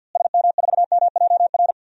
Morse code, heard as a keyed tone, about 52 wpm, 700 Hz. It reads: SM4MJR
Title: DXCC-100 Call Signs - 6 Characters - Repeated 2x - 50wpm